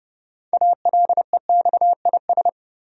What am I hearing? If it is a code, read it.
Morse code, heard as a keyed tone, about 30 wpm, 700 Hz. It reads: ALEXIS
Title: USA Common Names - 30wpm